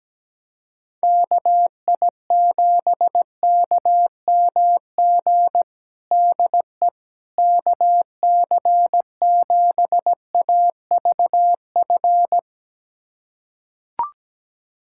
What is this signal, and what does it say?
Morse code, heard as a keyed tone, about 17 wpm, 700 Hz. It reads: KI7KMG DE KC7AVF
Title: USA Callsign DE Callsign - 6 Characters - 17wpm